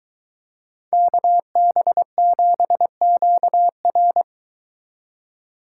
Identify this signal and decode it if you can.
Morse code, heard as a keyed tone, about 22 wpm, 700 Hz. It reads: KB7QR